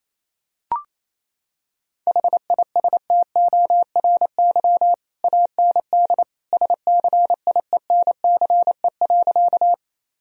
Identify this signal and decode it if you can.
Morse code, heard as a keyed tone, about 27 wpm, 700 Hz. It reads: HISTORY AND SCIENCE.
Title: Sentences from Top 1000 Words - Review Part 2 - 28wpm